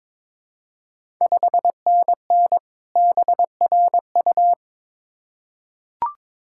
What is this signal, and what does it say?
Morse code, heard as a keyed tone, about 22 wpm, 700 Hz. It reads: 5NN BRU
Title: Ontario QSO Party Exchange 22wpm